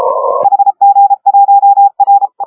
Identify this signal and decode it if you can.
Morse code, heard as a keyed tone, about 34 wpm, 810 Hz. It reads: HG1R
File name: AUDIO2019 / HG1R CW SSB MIX